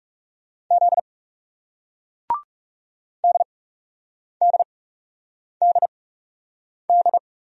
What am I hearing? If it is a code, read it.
Morse code, heard as a keyed tone, about 37 wpm, spaced as for 8 wpm, 700 Hz. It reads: Z DDDD
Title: Speed-Racing Single Letters 30wpm